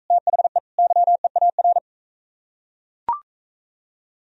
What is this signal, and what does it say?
Morse code, heard as a keyed tone, about 43 wpm, 700 Hz. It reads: THE YEAR